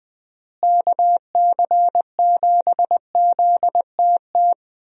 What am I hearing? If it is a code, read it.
Morse code, heard as a keyed tone, about 19 wpm, 700 Hz. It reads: KC7ZTT
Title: USA Call Signs - 6 Characters - Repeated 3x - 20wpm